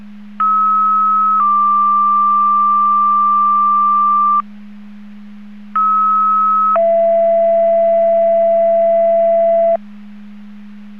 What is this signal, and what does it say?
Two-tone page, heard as a one-second tone, then a three-second tone, about 1290 then 1180 Hz, about 1290 then 670 Hz. It is heard from 0.4 to 9.8 s.